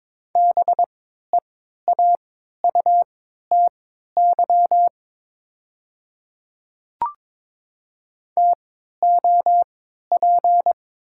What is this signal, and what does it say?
Morse code, heard as a keyed tone, about 22 wpm, spaced as for 12 wpm, 700 Hz. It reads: BEAUTY TOP
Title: 3x Character Spacing - Top 500 Words - 22wpm